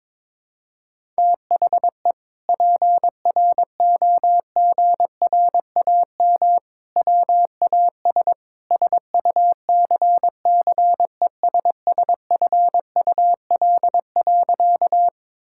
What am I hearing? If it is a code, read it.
Morse code, heard as a keyed tone, about 22 wpm, 700 Hz. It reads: THE PROGRAM WAS SUCCESSFUL.
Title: Sentences from Top 1000 Words - Review Part 5 - 22wpm